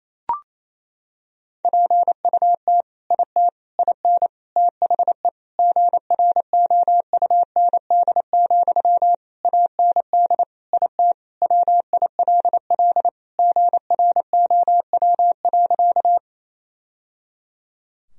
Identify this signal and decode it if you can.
Morse code, heard as a keyed tone, about 28 wpm, 700 Hz. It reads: PUT IT IN THE GROUND, AND IT WILL GROW.
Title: Sentences from Top 1000 Words - Review Part 2 - 28wpm